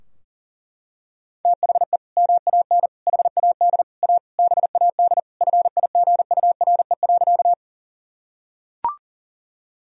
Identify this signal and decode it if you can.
Morse code, heard as a keyed tone, about 40 wpm, 700 Hz. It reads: THE MAN HAD A BAD FIGURE.
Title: Sentences from Top 1000 Words - Review Part 2 - 40wpm